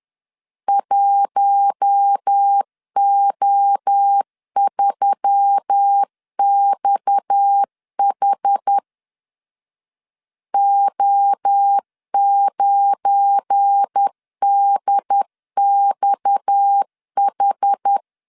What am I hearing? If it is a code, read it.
Morse code, heard as a keyed tone, about 10 wpm, 800 Hz. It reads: 1O3XH O9DXH
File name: LEZIONI / 7 GF9Z / 7 GF9Z 10wpm 2611